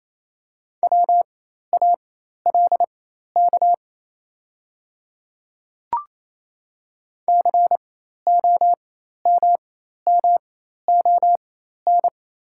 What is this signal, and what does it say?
Morse code, heard as a keyed tone, about 28 wpm, spaced as for 13 wpm, 700 Hz. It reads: WALK COMMON
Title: 4x Character Spacing - Top 500 Words - 28wpm